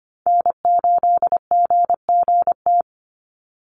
Morse code, heard as a keyed tone, about 25 wpm, 700 Hz. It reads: N8GGT
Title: USA Call Signs 25wpm